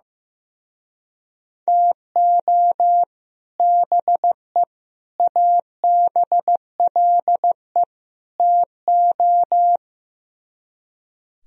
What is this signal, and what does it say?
Morse code, heard as a keyed tone, about 16 wpm, 700 Hz. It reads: TO BE ABLE TO
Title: Speed-Racing - Sets of 4 Words 15wpm